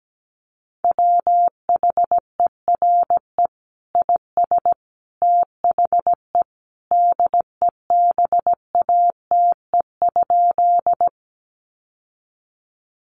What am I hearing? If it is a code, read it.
Morse code, heard as a keyed tone, about 17 wpm, 700 Hz. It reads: WHERE IS THE DEBATE?
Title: Sentences from Top 1100 Words 17wpm